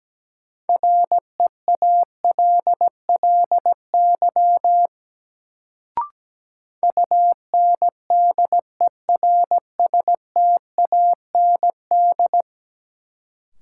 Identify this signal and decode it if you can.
Morse code, heard as a keyed tone, about 17 wpm, 700 Hz. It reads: REALLY UNDERSTAND
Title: Sets of 4 Words - New Word Drill - 17wpm